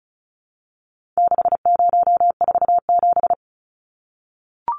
Morse code, heard as a keyed tone, about 34 wpm, 700 Hz. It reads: 6047
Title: Numbers - 4-Digit 35wpm